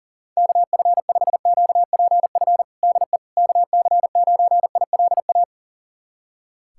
Morse code, heard as a keyed tone, about 40 wpm, 700 Hz. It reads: KF5QPF DE KC9ILA
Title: USA Callsign DE Callsign 40wpm